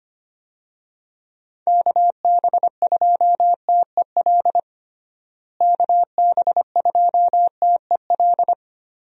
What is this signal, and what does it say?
Morse code, heard as a keyed tone, about 25 wpm, 700 Hz. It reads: KB2TEL KB2TEL
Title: USA Call Signs - 6 Characters - Repeated 3x - 25wpm